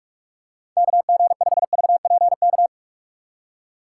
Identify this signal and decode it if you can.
Morse code, heard as a keyed tone, about 46 wpm, 700 Hz. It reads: KG5VPK
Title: USA Call Signs - 4x Character Spacing - 45wpm